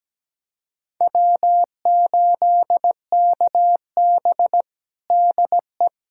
Morse code, heard as a keyed tone, about 17 wpm, 700 Hz. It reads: W8KB DE
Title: USA Callsign DE Callsign - 4 Characters - 17wpm